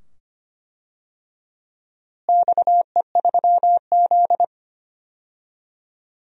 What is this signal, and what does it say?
Morse code, heard as a keyed tone, about 25 wpm, 700 Hz. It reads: XE3Z
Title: DXCC-100 Call Signs - 4 Characters - Repeated 2x - 25wpm